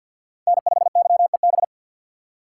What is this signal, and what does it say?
Morse code, heard as a keyed tone, about 47 wpm, 700 Hz. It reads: N5YEB
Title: USA Call Signs - 3x Character Spacing - 50wpm